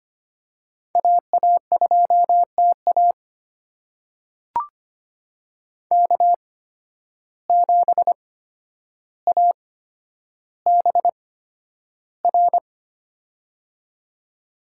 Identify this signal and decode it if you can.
Morse code, heard as a keyed tone, about 25 wpm, spaced as for 11 wpm, 700 Hz. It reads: AA2TA K7ABR
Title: USA Call Signs - 5 Characters - 8x Character Spacing - 25wpm